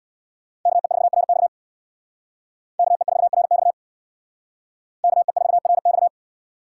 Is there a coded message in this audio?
Yes: Morse code from 0.7 to 6.1 s, heard as a keyed tone, about 64 wpm, 700 Hz. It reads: KE4RX KE4RX KE4RX